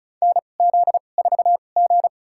Morse code, heard as a keyed tone, about 36 wpm, spaced as for 25 wpm, 700 Hz. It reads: NZ4G
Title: USA Call Signs - 4 Characters - 2x Character Spacing - 35wpm